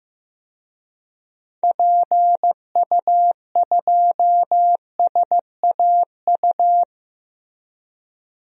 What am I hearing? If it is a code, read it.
Morse code, heard as a keyed tone, about 15 wpm, 700 Hz. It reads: PU2SAU
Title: DXCC-100 Call Signs - 6 Characters - Repeated 2x - 15wpm